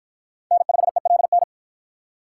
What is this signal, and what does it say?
Morse code, heard as a keyed tone, about 48 wpm, 700 Hz. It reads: N5ELN